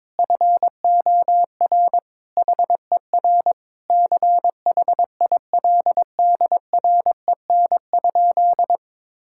Morse code, heard as a keyed tone, about 22 wpm, 700 Hz. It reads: FOR HER CHILDREN?
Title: Sentences from Top 1000 Words - Review Part 6 - 22wpm